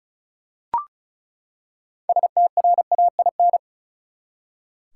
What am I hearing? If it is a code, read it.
Morse code, heard as a keyed tone, about 36 wpm, 700 Hz. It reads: STRAIN
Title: Six Letter Words - 35wpm